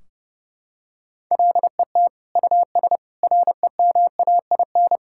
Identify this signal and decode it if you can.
Morse code, heard as a keyed tone, about 30 wpm, 700 Hz. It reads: LET US REMAIN
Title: Sentences from Top 1600 Words 30wpm